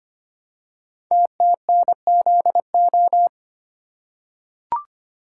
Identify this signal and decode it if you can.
Morse code, heard as a keyed tone, about 25 wpm, 700 Hz. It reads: TTNZO
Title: Code Groups - 5 Letters - 25wpm